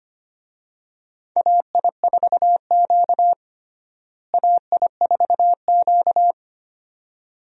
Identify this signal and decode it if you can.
Morse code, heard as a keyed tone, about 25 wpm, 700 Hz. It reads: AI4Q AI4Q
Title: USA Call Signs - 4 Characters - Repeated 2x - 25wpm